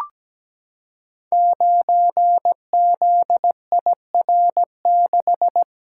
Morse code, heard as a keyed tone, about 17 wpm, 700 Hz. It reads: 9ZIR6
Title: Code Groups - 5 Letter:Numbers - 17wpm